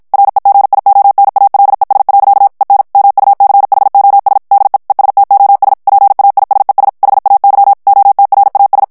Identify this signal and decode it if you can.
Morse code, heard as a keyed tone, about 56 wpm, 800 Hz. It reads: REGIONALES. EN MUCHOS DE ESTOS PAISES HAY OTRAS
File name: ESPAGNOL TEXTE EN CLAIR 1 VITESSE 50 MOTS MINUTES